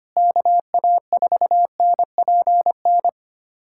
Morse code, heard as a keyed tone, about 25 wpm, 700 Hz. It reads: KA4NPN
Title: USA Callsign DE Callsign - 6 Characters - 25wpm